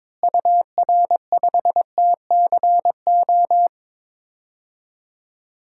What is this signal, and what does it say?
Morse code, heard as a keyed tone, about 21 wpm, 700 Hz. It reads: UR5TCO
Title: DXCC-100 Call Signs - 6 Characters - Repeated 2x - 22wpm